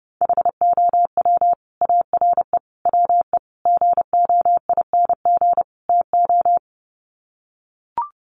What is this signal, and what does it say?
Morse code, heard as a keyed tone, about 30 wpm, 700 Hz. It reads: HOW ARE WE GOING TO
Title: Sets of 5 Words Encore 30wpm